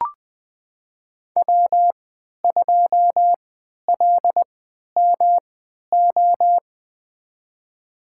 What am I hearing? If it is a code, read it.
Morse code, heard as a keyed tone, about 20 wpm, spaced as for 11 wpm, 700 Hz. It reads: W2LMO
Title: USA Call Signs - 5 Characters - 3x Character Spacing - 20wpm